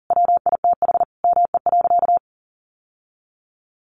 Morse code, heard as a keyed tone, about 40 wpm, 700 Hz. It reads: WITH ME.
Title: Sentences from Top 1500 Words 40wpm